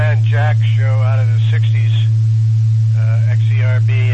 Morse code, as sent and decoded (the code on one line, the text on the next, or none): none